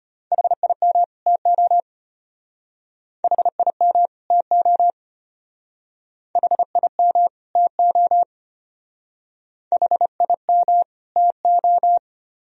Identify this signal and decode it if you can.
Morse code, heard as a keyed tone, about 31 wpm, 700 Hz. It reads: HIM TO HIM TO HIM TO HIM TO